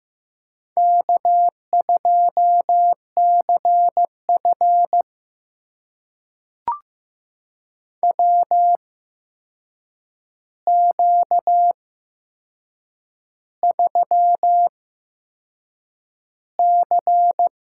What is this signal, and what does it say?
Morse code, heard as a keyed tone, about 15 wpm, spaced as for 6 wpm, 700 Hz. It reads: K2CF WQ3C
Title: USA Call Signs - 4 Characters - 8x Character Spacing - 15wpm